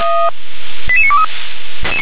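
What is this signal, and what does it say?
Five-tone selective call (ZVEI), heard as quick runs of five tones, tones of about 70 ms. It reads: 88923
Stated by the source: ZVEI five-tone selective call 88778 received by SDR.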